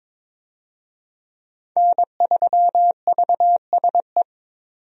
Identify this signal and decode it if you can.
Morse code, heard as a keyed tone, about 22 wpm, 700 Hz. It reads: N3VSE
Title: USA Call Signs - 5 Characters - 22wpm